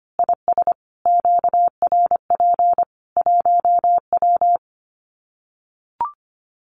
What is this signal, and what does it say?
Morse code, heard as a keyed tone, about 25 wpm, 700 Hz. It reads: IS QRP 1W
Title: Common QSO Phrases 25wpm